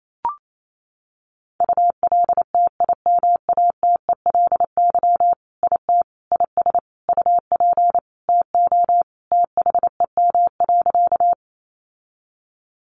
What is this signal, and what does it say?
Morse code, heard as a keyed tone, about 28 wpm, 700 Hz. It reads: ULTIMATELY IT IS UP TO THEM.
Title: Sentences from Top 1200 Words 28wpm